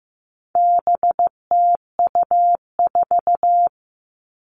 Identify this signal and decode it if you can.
Morse code, heard as a keyed tone, about 15 wpm, 700 Hz. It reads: BTU4
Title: All Cut Numbers - 4-Digit 15wpm